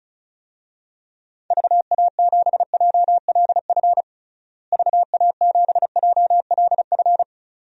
Morse code, heard as a keyed tone, about 36 wpm, 700 Hz. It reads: VA7JLF VA7JLF
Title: DXCC-100 Call Signs - 6 Characters - Repeated 2x - 35wpm